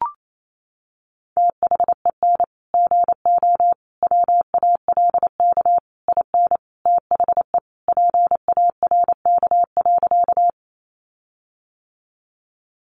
Morse code, heard as a keyed tone, about 28 wpm, 700 Hz. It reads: THEN GO WALK IN THE PARK.
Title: Sentences from Top 1300 Words 28wpm